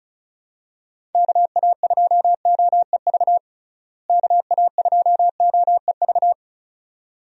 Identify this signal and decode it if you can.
Morse code, heard as a keyed tone, about 34 wpm, 700 Hz. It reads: KA2OEV KA2OEV
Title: USA Call Signs - 6 Characters - Repeated 2x - 35wpm